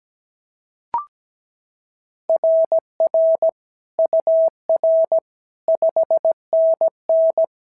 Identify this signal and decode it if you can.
Morse code, heard as a keyed tone, about 16 wpm, 650 Hz. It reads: RR UR 5NN
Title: SOTA QSO Practice 17wpm